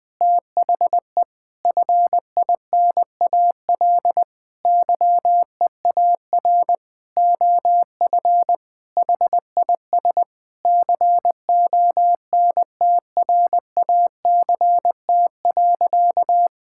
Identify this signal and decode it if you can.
Morse code, heard as a keyed tone, about 20 wpm, 700 Hz. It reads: THE FINAL YEAR OF HIS CONTRACT.
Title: Sentences with 6-Word Phrases - 20wpm